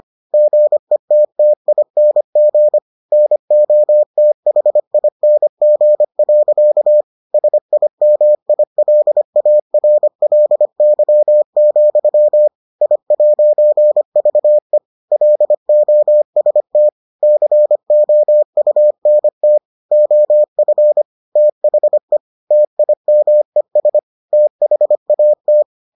Morse code, heard as a keyed tone, about 25 wpm, 600 Hz. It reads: GETTING NOTHING. SIMILARLY, I'VE LOST COUNT OF THE TIMES THAT